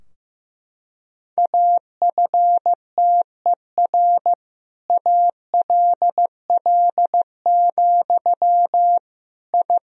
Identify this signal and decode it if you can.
Morse code, heard as a keyed tone, about 15 wpm, 700 Hz. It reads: AFTER ALL, I
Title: Sentences from Top 1000 Words - Review Part 3 - 15wpm